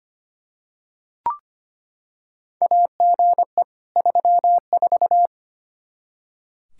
Morse code, heard as a keyed tone, about 25 wpm, 700 Hz. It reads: AGE 34
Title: Practice QSO - Exchange 3 - More Info - 25wpm